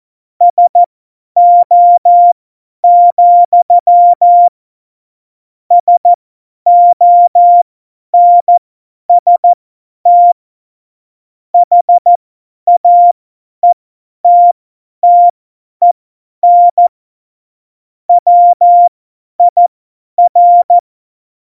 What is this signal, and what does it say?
Morse code, heard as a keyed tone, about 14 wpm, spaced as for 10 wpm, 700 Hz. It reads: SO, SONST HAETTEN WIR
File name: Morsekurs / Witze / Witz 24.Dezember 2019, 50 F70